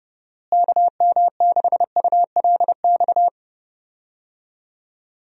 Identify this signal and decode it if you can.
Morse code, heard as a keyed tone, about 30 wpm, 700 Hz. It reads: KM6ULX
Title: USA Call Signs - 6 Characters - Repeated 2x - 30wpm